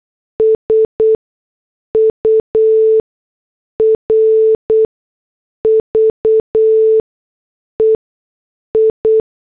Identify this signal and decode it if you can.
Morse code, heard as a keyed tone, about 8 wpm, 440 Hz. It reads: S U R V E I
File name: morse / headlines